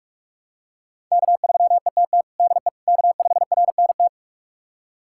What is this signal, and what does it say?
Morse code, heard as a keyed tone, about 46 wpm, 700 Hz. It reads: K3ETT DE K5RNT
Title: USA Callsign DE Callsign - 5 Characters - 45wpm